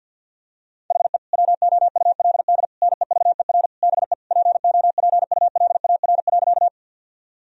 Morse code, heard as a keyed tone, about 50 wpm, 700 Hz. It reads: HE WOULD NEVER BE POPULAR.
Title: Sentences from Top 1000 Words - Review Part 3 - 50wpm